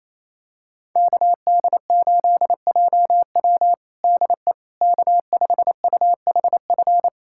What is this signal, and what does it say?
Morse code, heard as a keyed tone, about 28 wpm, 700 Hz. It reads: KD8JW DE K5UHF
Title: USA Callsign DE Callsign - 5 Characters - 28wpm